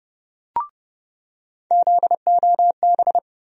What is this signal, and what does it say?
Morse code, heard as a keyed tone, about 31 wpm, 700 Hz. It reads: ZOB
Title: USA Call Sign Postfixes - 3 Letters 30wpm